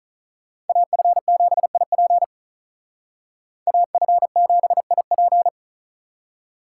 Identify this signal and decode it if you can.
Morse code, heard as a keyed tone, about 46 wpm, 700 Hz. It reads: AF7IP AF7IP